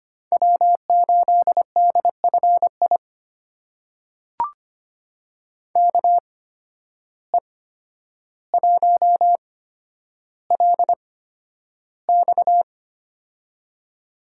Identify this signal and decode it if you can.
Morse code, heard as a keyed tone, about 25 wpm, spaced as for 11 wpm, 700 Hz. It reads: W8DFI KE1LX
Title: USA Call Signs - 5 Characters - 8x Character Spacing - 25wpm